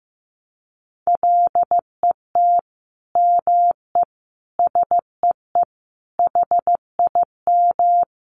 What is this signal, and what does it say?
Morse code, heard as a keyed tone, about 15 wpm, 700 Hz. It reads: LET ME SEE HIM
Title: Sentences from Top 300 Words 15wpm